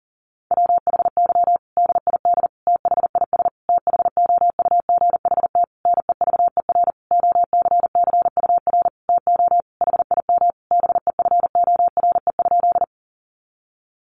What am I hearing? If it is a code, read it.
Morse code, heard as a keyed tone, about 37 wpm, 700 Hz. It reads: WHY DID THIS THOUGHT NEVER OCCUR TO HIM BEFORE?
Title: Sentences from Top 900 Words 40wpm